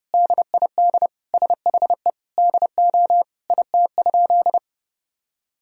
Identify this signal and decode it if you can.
Morse code, heard as a keyed tone, about 30 wpm, 700 Hz. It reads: DID SHE DO IT?
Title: Sentences from Top 100 Words and Abbreviations 30wpm